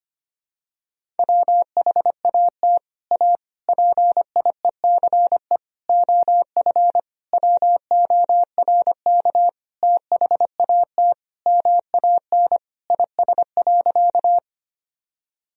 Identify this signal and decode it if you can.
Morse code, heard as a keyed tone, about 25 wpm, 700 Hz. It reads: WHAT A PIECE OF WORK THAT MAN IS.
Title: Sentences from Top 1000 Words - Review Part 2 - 25wpm